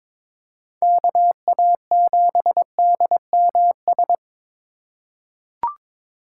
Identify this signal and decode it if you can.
Morse code, heard as a keyed tone, about 22 wpm, 700 Hz. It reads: KA7DMS
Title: USA Call Signs - 8x Character Spacing - 22wpm